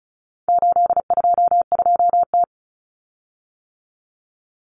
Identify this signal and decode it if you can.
Morse code, heard as a keyed tone, about 34 wpm, 700 Hz. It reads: 822T
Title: Common Cut Numbers - 4-Digit 35wpm